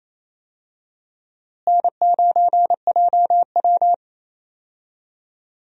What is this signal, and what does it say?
Morse code, heard as a keyed tone, about 28 wpm, 700 Hz. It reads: N9JW
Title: USA Call Signs - 4 Characters - Repeated 3x - 28wpm